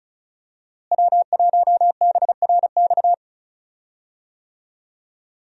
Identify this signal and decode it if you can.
Morse code, heard as a keyed tone, about 34 wpm, 700 Hz. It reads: W1BRX
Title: USA Call Signs - 5 Characters - Repeated 3x - 35wpm